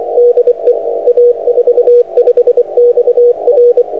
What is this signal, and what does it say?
Morse code, heard as a keyed tone, about 24 wpm, 480 Hz, with radio noise. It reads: DE A45XR
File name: a45xr-6m